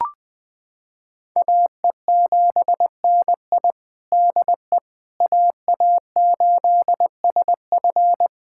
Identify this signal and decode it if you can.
Morse code, heard as a keyed tone, about 20 wpm, 700 Hz. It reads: AE7NI DE AA8SF
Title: USA Callsign DE Callsign - 5 Characters - 20wpm